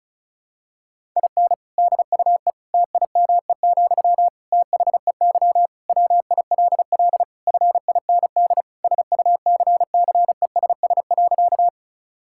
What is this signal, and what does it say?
Morse code, heard as a keyed tone, about 36 wpm, 700 Hz. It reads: IN DUE TIME, THEY WILL FIND SUCCESS.
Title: Sentences from Top 1000 Words - Review Part 6 - 35wpm